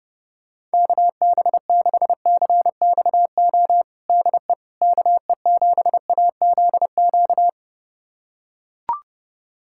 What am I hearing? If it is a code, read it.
Morse code, heard as a keyed tone, about 32 wpm, 700 Hz. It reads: KB6CXO DE KE7AZQ
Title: USA Callsign DE Callsign - 6 Characters - 30wpm